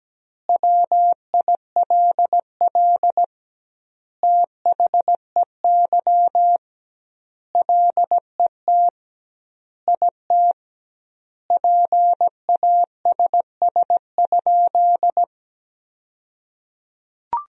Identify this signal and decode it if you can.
Morse code, heard as a keyed tone, about 17 wpm, 700 Hz. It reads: WILL THEY LET IT PASS?
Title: 2x Word-Spacing - Sentences from Top 400 Words - 17wpm